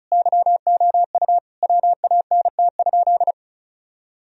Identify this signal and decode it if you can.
Morse code, heard as a keyed tone, about 36 wpm, 700 Hz. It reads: YOU WANT?
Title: Sentences from Top 1000 Words - Review Part 4 - 35wpm